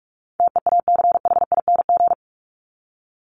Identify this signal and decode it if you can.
Morse code, heard as a keyed tone, about 44 wpm, 700 Hz. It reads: TEACHING
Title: Top 1001-1250 Words 45wpm